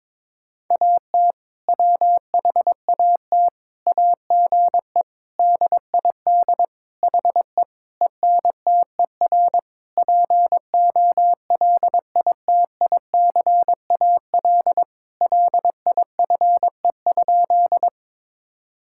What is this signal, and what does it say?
Morse code, heard as a keyed tone, about 22 wpm, 700 Hz. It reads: AT WHAT AGE DID HE ENTER POLITICAL LIFE?
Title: Sentences from Top 1000 Words - Review Part 5 - 22wpm